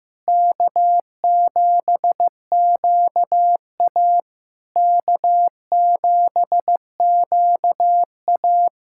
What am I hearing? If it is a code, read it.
Morse code, heard as a keyed tone, about 15 wpm, 700 Hz. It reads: K7QA K7QA
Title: SOTA QSO Practice 15wpm